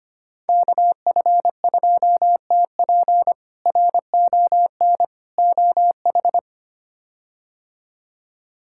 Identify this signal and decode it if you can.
Morse code, heard as a keyed tone, about 25 wpm, 700 Hz. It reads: KF2TP RON OH
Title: K1USN SST QSO Practice - 25wpm